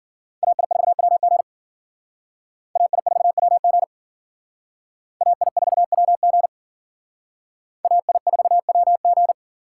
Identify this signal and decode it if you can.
Morse code, heard as a keyed tone, about 51 wpm, 690 Hz. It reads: AI4WG AI4WG AI4WG AI4WG